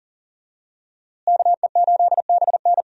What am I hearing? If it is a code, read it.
Morse code, heard as a keyed tone, about 37 wpm, 700 Hz. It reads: KE8BN
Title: USA Call Signs - 5 Characters - Repeated 3x - 40wpm